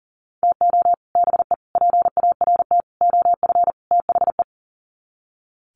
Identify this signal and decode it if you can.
Morse code, heard as a keyed tone, about 40 wpm, 700 Hz. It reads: TO BE PART OF THE
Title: Sets of 5 Words 40wpm